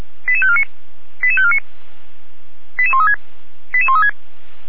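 Five-tone selective call (ZVEI), heard as quick runs of five tones, tones of about 70 ms. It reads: 88549 88549 88136 88136
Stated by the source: ZVEI five-tone selective call 88190 received on an SDR.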